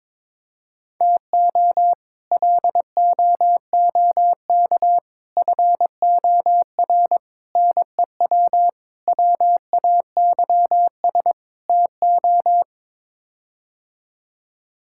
Morse code, heard as a keyed tone, about 23 wpm, 700 Hz. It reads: TO LOOK FOR NEW WAYS TO
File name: practice / 22wpm / Sets of 6 Words - 22wpm